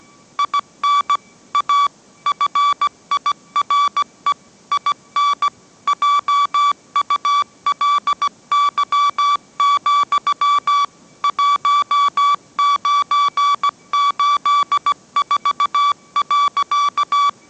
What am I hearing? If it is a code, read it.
Morse code, heard as a keyed tone, about 17 wpm, 1150 Hz. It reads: IN A FIRE IN JULY, 1984.
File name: mrx / mrx080 18wpm